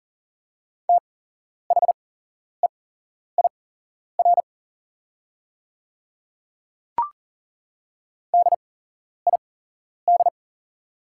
Morse code, heard as a keyed tone, about 40 wpm, spaced as for 11 wpm, 690 Hz. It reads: THEIR DID
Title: 8x Character Spacing - Top 200 Words - 40wpm